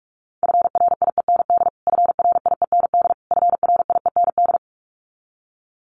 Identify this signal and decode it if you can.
Morse code, heard as a keyed tone, about 46 wpm, 700 Hz. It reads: FRIEND FRIEND FRIEND
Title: Top 300 Words Repeated 45wpm